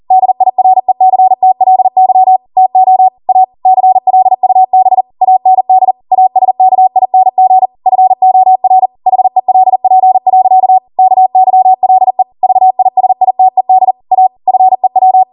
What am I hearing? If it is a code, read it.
Morse code, heard as a keyed tone, about 40 wpm, 750 Hz. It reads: DIRECTLY TO A CLUB AND ASKING FOR HELP. KYLE VISITED A FEW